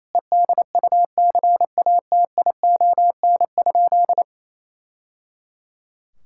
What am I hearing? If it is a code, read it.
Morse code, heard as a keyed tone, about 28 wpm, 700 Hz. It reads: EDUCATION?
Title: Sentences from Top 1000 Words - Review Part 6 - 28wpm